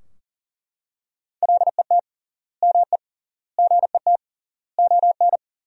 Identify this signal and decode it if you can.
Morse code, heard as a keyed tone, about 40 wpm, 700 Hz. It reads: LET ME GET ON
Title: 3x Word-Spacing - Sentences from Top 300 Words - 40wpm